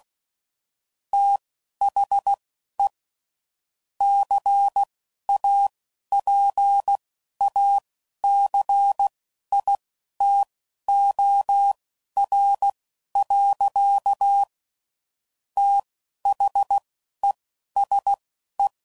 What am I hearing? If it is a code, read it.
Morse code, heard as a keyed tone, about 16 wpm, spaced as for 11 wpm, 800 Hz. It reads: THE CAPACITOR. THESE
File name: CW Code Groups @10WPM / Technical Subjects / Reactance